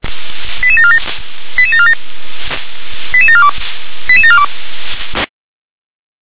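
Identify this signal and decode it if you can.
Five-tone selective call (ZVEI), heard as quick runs of five tones, tones of about 70 ms. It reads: 88747 88747 88742 88742
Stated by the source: Five-tone ZVEI selective call received on an SDR.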